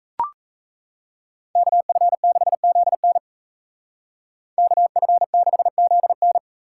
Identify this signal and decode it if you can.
Morse code, heard as a keyed tone, about 36 wpm, 700 Hz. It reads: KF6ZN KF6ZN